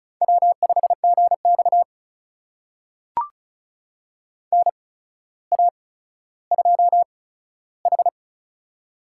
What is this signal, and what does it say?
Morse code, heard as a keyed tone, about 36 wpm, spaced as for 15 wpm, 700 Hz. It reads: W5GX NA2H
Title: USA Call Signs - 4 Characters - 8x Character Spacing - 35wpm